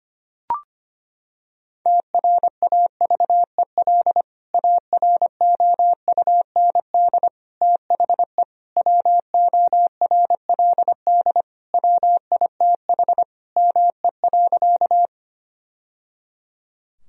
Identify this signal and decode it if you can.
Morse code, heard as a keyed tone, about 25 wpm, 700 Hz. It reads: TRAVEL AROUND THE WORLD WITH ME.
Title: Sentences from Top 1000 Words - Review Part 3 - 25wpm